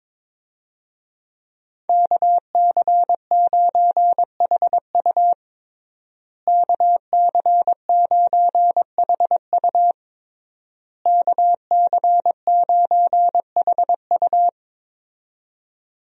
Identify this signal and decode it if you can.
Morse code, heard as a keyed tone, about 22 wpm, 700 Hz. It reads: KC9HU KC9HU KC9HU
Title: USA Call Signs - 5 Characters - Repeated 3x - 22wpm